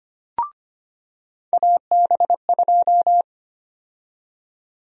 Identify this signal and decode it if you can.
Morse code, heard as a keyed tone, about 25 wpm, 700 Hz. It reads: AB2
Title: USA Call Sign Prefixes - 25wpm